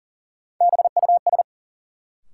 Morse code, heard as a keyed tone, about 39 wpm, 700 Hz. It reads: BUS
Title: Sentences from Top 1000 Words - Review Part 6 - New Word Drill - 40wpm